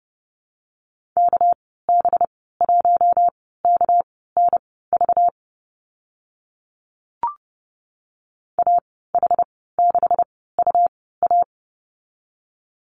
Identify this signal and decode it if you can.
Morse code, heard as a keyed tone, about 30 wpm, spaced as for 17 wpm, 700 Hz. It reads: KB1KNV AH6UA
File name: Lesson 37g - Letter x - Call Signs 30wpm